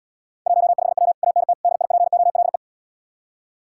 Morse code, heard as a keyed tone, about 80 wpm, 700 Hz. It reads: 15W ANT DIPOLE